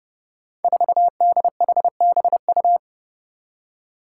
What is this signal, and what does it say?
Morse code, heard as a keyed tone, about 30 wpm, 700 Hz. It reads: 4DHBU
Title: Five Challenging Letter-Number Combinations 30wpm